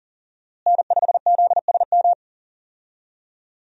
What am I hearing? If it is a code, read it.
Morse code, heard as a keyed tone, about 40 wpm, 700 Hz. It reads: N5ZSM
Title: USA Call Signs - 5 Characters - Repeated 3x - 40wpm